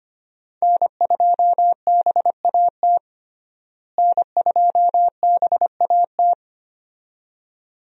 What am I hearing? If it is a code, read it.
Morse code, heard as a keyed tone, about 25 wpm, 700 Hz. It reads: N2BAT N2BAT
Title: USA Call Signs - 5 Characters - Repeated 2x - 25wpm